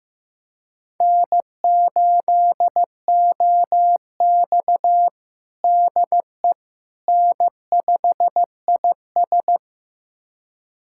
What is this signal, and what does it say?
Morse code, heard as a keyed tone, about 15 wpm, 700 Hz. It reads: N8OX DE N5IS
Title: USA Callsign DE Callsign - 4 Characters - 15wpm